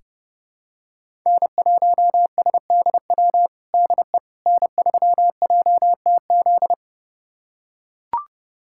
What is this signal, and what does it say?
Morse code, heard as a keyed tone, about 31 wpm, 700 Hz. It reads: N1SDW DE N3JTZ
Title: USA Callsign DE Callsign - 5 Characters - 30wpm